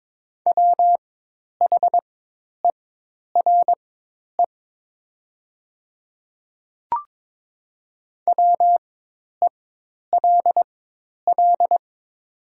Morse code, heard as a keyed tone, about 22 wpm, spaced as for 10 wpm, 700 Hz. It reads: WHERE WELL
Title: 4x Character Spacing - Top 200 Words - 22wpm